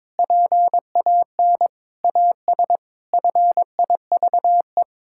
Morse code, heard as a keyed tone, about 22 wpm, 700 Hz. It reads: PAN AS FIVE
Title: Pangram Challenge v2 - 22wpm